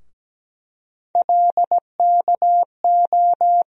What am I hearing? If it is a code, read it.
Morse code, heard as a keyed tone, about 17 wpm, 700 Hz. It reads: LKO
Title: USA Call Sign Postfixes - 3 Letters 17wpm